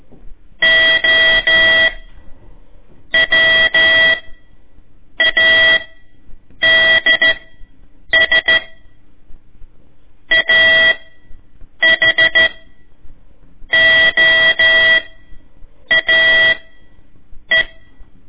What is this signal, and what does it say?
Morse code, heard as a keyed tone, about 14 wpm, spaced as for 7 wpm, 1950 Hz. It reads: OWADS AHOAE
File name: ik1weg2 / CORSI CW / lez4-1